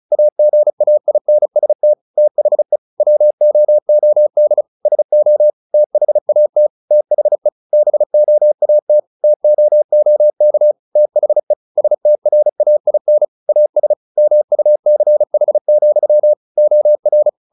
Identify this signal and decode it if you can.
Morse code, heard as a keyed tone, about 34 wpm, 600 Hz. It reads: AGAINST THE WOOD SO THAT THE BOAT TOOK THE STRAIN AS MUCH, OR
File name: the old man and the sea 2 35wpm